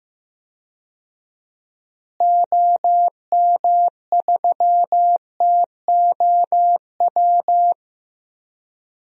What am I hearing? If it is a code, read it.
Morse code, heard as a keyed tone, about 16 wpm, 700 Hz. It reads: OM3TOW
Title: DXCC-100 Call Signs - 6 Characters - Repeated 2x - 15wpm